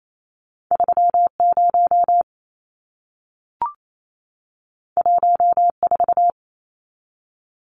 Morse code, heard as a keyed tone, about 28 wpm, 700 Hz. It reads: 30 14
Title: Numbers - 2-Digit 28wpm